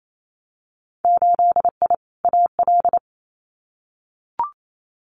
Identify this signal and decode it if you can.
Morse code, heard as a keyed tone, about 28 wpm, 700 Hz. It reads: 8I AL
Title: Winter Field Day Exchanges Encore 28wpm